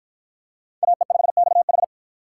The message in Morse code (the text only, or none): AE5XH